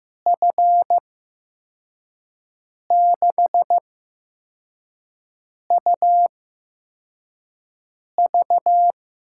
Morse code, heard as a keyed tone, about 15 wpm, spaced as for 4 wpm, 700 Hz. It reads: F6UV
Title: USA Call Signs - 6 Characters - 8x Character Spacing - 15wpm